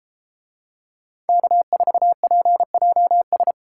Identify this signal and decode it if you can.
Morse code, heard as a keyed tone, about 33 wpm, 700 Hz. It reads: K4PJS